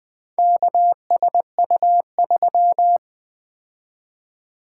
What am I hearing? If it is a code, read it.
Morse code, heard as a keyed tone, about 20 wpm, 700 Hz. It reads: KSU3
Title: Four Letter-Number Combinations - 20wpm